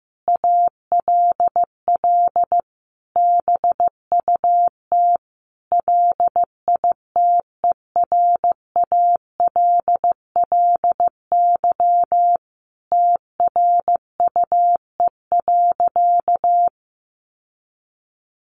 Morse code, heard as a keyed tone, about 15 wpm, 700 Hz. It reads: ALL BUT LITERALLY TRUE.
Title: Sentences from Top 1400 Words 15wpm